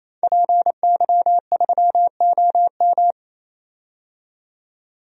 Morse code, heard as a keyed tone, about 28 wpm, 700 Hz. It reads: PY3OM
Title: DXCC-100 Call Signs - 5 Characters - 28wpm